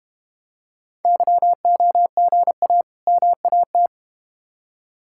Morse code, heard as a keyed tone, about 31 wpm, 700 Hz. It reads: YOGA MAT
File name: Easy Words - Speed-Racing with Hints - 25wpm